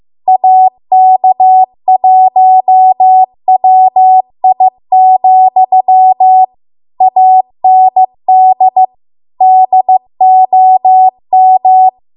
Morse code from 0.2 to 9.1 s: AK1WI, AND